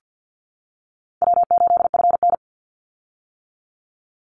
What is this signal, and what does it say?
Morse code, heard as a keyed tone, about 50 wpm, 700 Hz. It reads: W8FN
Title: Easy Call Signs - 50wpm